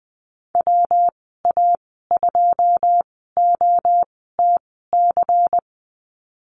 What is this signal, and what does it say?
Morse code, heard as a keyed tone, about 20 wpm, spaced as for 15 wpm, 700 Hz. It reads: WA2OTC
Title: 140 Lesson 23g - Letter f - Call Signs 20wpm